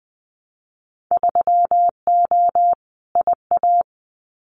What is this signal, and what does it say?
Morse code, heard as a keyed tone, about 20 wpm, 700 Hz. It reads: 3O IA
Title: Winter Field Day Exchanges 20wpm